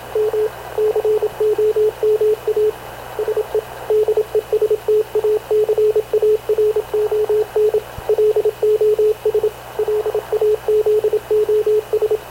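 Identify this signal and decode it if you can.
Morse code, heard as a keyed tone, about 27 wpm, 430 Hz, with radio noise. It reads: M COMA SE DESTACARON LOS LAZOS
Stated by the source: Havana Radio CLA press (PX) broadcast, recorded by Radio Officer Paul Dumesnil aboard mv CNAV QUEST/CZDO.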